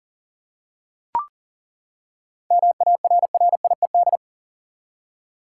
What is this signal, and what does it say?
Morse code, heard as a keyed tone, about 37 wpm, 700 Hz. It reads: MARRIED
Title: Seven Letter Words - 40wpm